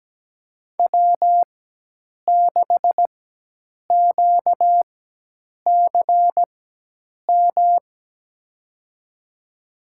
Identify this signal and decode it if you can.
Morse code, heard as a keyed tone, about 17 wpm, spaced as for 8 wpm, 700 Hz. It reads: W6QCM
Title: USA Call Signs - 5 Characters - 4x Character Spacing - 17wpm